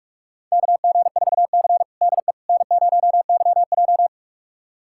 Morse code, heard as a keyed tone, about 46 wpm, 700 Hz. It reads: KG4C DE N0YJ
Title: USA Callsign DE Callsign - 4 Characters - 45wpm